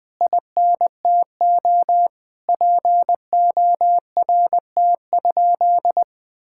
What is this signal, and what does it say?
Morse code, heard as a keyed tone, about 20 wpm, 700 Hz. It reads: INTO PORT?
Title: Sentences from Top 1000 Words - Review Part 5 - 20wpm